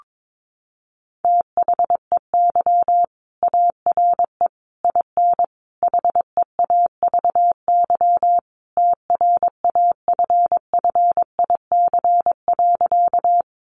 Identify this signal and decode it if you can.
Morse code, heard as a keyed tone, about 22 wpm, 700 Hz. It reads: THEY ARE IN HEAVY TRAFFIC.
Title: Sentences from Top 1300 Words 22wpm